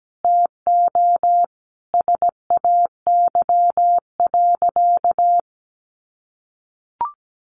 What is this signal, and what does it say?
Morse code, heard as a keyed tone, about 17 wpm, 700 Hz. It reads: TO SAY.
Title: Sentences from Top 1400 Words 17wpm